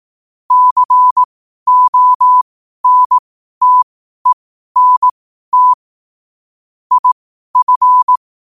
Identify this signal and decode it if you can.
Morse code, heard as a keyed tone, about 18 wpm, spaced as for 12 wpm, 1000 Hz. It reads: CONTENT IF
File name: qotdfiles / 12wpm